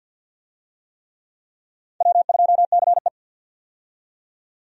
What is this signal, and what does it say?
Morse code, heard as a keyed tone, about 53 wpm, 700 Hz. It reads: W2CE
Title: USA Call Signs - 4 Characters - 50wpm